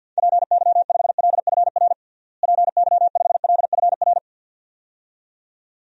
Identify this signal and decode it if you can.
Morse code, heard as a keyed tone, about 45 wpm, 700 Hz. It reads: PY5LFR PY5LFR
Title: DXCC-100 Call Signs - 6 Characters - Repeated 2x - 50wpm